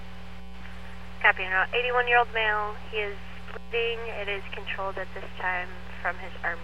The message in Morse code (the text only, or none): none